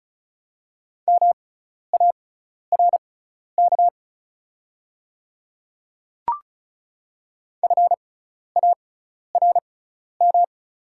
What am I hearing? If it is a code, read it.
Morse code, heard as a keyed tone, about 34 wpm, spaced as for 12 wpm, 690 Hz. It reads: MARK FARM
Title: 6x Character Spacing - Top 500 Words - 35wpm